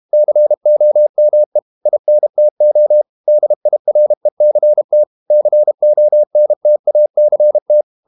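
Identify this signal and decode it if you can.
Morse code, heard as a keyed tone, about 32 wpm, 600 Hz. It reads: COME INTO DIRECT CONTACT
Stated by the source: Morse code practice at 32 wpm from CW med Gnister, reading Ken Follett's novel Never in English.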